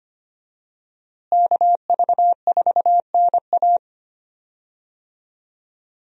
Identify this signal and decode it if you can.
Morse code, heard as a keyed tone, about 25 wpm, 700 Hz. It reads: KV4NA
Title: USA Call Signs - 5 Characters - Repeated 2x - 25wpm